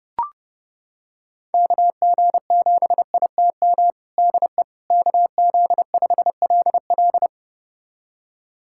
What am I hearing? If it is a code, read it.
Morse code, heard as a keyed tone, about 32 wpm, 700 Hz. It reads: KG7ITM DE KZ5LL
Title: USA Callsign DE Callsign 30wpm